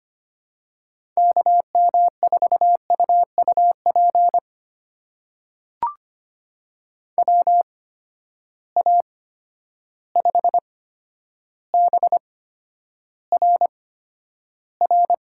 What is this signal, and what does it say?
Morse code, heard as a keyed tone, about 25 wpm, spaced as for 11 wpm, 700 Hz. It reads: KM4UUP WA5BRR
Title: USA Call Signs - 6 Characters - 8x Character Spacing - 25wpm